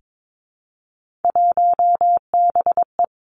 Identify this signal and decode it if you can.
Morse code, heard as a keyed tone, about 22 wpm, 700 Hz. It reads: 1BE
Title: Three Challenging Letter-Number Combinations 22wpm